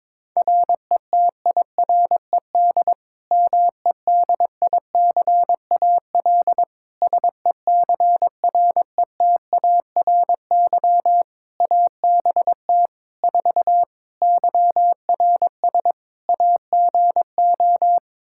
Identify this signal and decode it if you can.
Morse code, heard as a keyed tone, about 22 wpm, 700 Hz. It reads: RETIRED MEDICAL SECRETARY ABT 4 YRS AGO
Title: Practice QSO - Exchange 3 - More Info - 22wpm